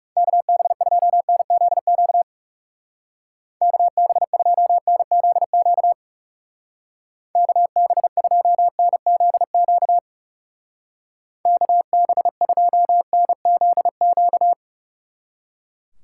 Morse code, heard as a keyed tone, about 37 wpm, 700 Hz. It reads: KB2NZQ KB2NZQ KB2NZQ KB2NZQ
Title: Speed Racing - USA 2x3 Call Signs 30wpm